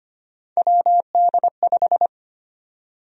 Morse code, heard as a keyed tone, about 25 wpm, 700 Hz. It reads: WD5
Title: USA Call Sign Prefixes - Repeated 2x - 25wpm